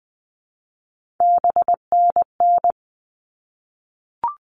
Morse code, heard as a keyed tone, about 20 wpm, 700 Hz. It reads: BNN
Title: All Cut Numbers - 3-Digit 20wpm